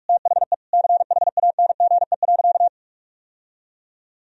Morse code, heard as a keyed tone, about 46 wpm, 700 Hz. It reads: THE CHANGE.
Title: Sentences from Top 1000 Words - Review Part 1 - 45wpm